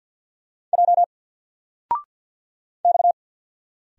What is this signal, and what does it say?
Morse code, heard as a keyed tone, about 50 wpm, 700 Hz. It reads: J X
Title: Single Letter-Number Alt 50wpm